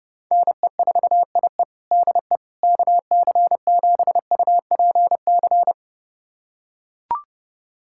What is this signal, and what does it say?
Morse code, heard as a keyed tone, about 31 wpm, 700 Hz. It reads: NE4IE DE KC7UPC
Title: USA Callsign DE Callsign 30wpm